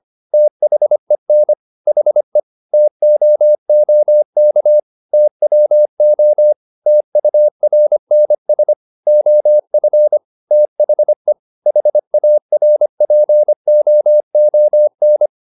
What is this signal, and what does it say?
Morse code, heard as a keyed tone, about 25 wpm, 600 Hz. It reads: THEN HE TOOK TWO TURNS OF THE HARPOON